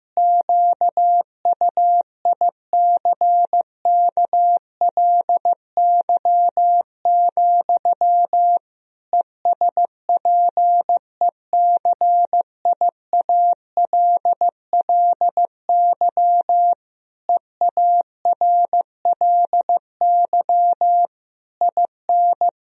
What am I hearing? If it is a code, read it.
Morse code, heard as a keyed tone, about 16 wpm, 700 Hz. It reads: QUICKLY, ESPECIALLY EARLY IN
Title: Sentences from Top 1000 Words - Review Part 4 - 15wpm